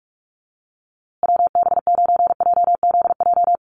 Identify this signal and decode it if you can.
Morse code, heard as a keyed tone, about 44 wpm, 690 Hz. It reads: WB9JZJ
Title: USA Call Signs Repeated 3x 45wpm